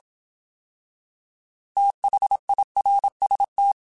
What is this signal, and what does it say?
Morse code, heard as a keyed tone, about 27 wpm, 800 Hz. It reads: THIRST
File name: TH / TH mp3s / Fast TH @25